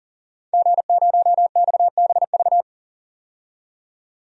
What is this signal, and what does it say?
Morse code, heard as a keyed tone, about 40 wpm, 700 Hz. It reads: G0XBV
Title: DXCC-100 Call Signs - 5 Characters - 40wpm